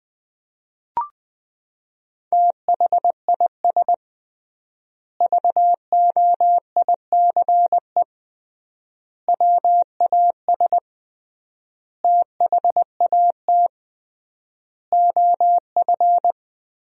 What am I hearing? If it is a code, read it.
Morse code, heard as a keyed tone, about 20 wpm, 700 Hz. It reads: THIS VOICE WAS THAT OF
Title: 3x Word-Spacing - Sentences from Top 500 Words - 20wpm